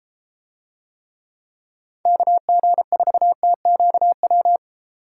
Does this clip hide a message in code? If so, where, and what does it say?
Yes: Morse code from 2.1 to 4.6 s, heard as a keyed tone, about 32 wpm, 700 Hz. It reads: KG4TQW